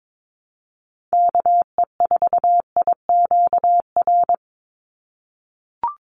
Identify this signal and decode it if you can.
Morse code, heard as a keyed tone, about 22 wpm, 700 Hz. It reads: KE4IQR
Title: USA Call Signs 22wpm